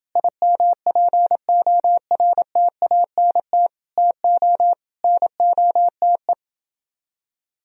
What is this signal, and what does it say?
Morse code, heard as a keyed tone, about 29 wpm, 700 Hz. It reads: IMPORTANT TO NOTE